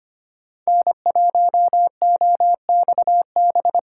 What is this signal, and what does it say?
Morse code, heard as a keyed tone, about 25 wpm, 700 Hz. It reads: N1OXB
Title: USA Call Signs - 5 Characters - 3x Character Spacing - 25wpm